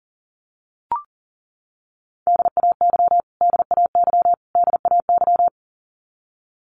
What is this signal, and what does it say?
Morse code, heard as a keyed tone, about 40 wpm, 700 Hz. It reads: DAY DAY DAY
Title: Top 500 Words Repeated 40wpm